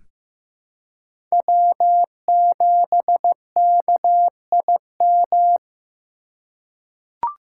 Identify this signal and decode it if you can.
Morse code, heard as a keyed tone, about 15 wpm, 700 Hz. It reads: W7KIM
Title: USA Call Signs - 5 Characters - 6x Character Spacing - 15wpm